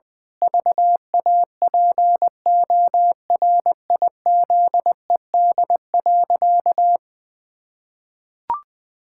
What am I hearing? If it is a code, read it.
Morse code, heard as a keyed tone, about 20 wpm, 700 Hz. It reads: VAPORIZED.
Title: Pangram Challenge v2 - 20wpm